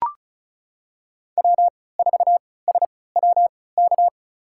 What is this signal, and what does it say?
Morse code, heard as a keyed tone, about 36 wpm, spaced as for 20 wpm, 700 Hz. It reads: W4SWK
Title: USA Call Signs - 5 Characters - 3x Character Spacing - 35wpm